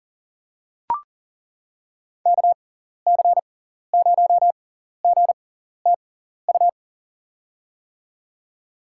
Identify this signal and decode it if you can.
Morse code, heard as a keyed tone, about 37 wpm, spaced as for 14 wpm, 700 Hz. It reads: KC0GTU
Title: USA Call Signs - 6 Characters - 6x Character Spacing - 40wpm